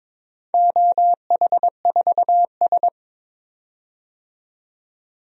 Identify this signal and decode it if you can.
Morse code, heard as a keyed tone, about 22 wpm, 700 Hz. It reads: OH4S
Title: DXCC-100 Call Signs - 4 Characters - 22wpm